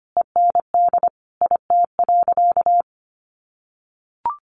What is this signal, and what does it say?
Morse code, heard as a keyed tone, about 25 wpm, 700 Hz. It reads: END IT.
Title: Sentences from Top 1200 Words 25wpm